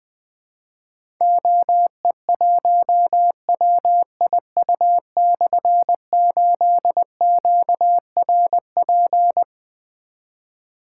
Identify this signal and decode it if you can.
Morse code, heard as a keyed tone, about 20 wpm, 700 Hz. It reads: OE1WIU/8QRP
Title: Difficult Call Signs 20wpm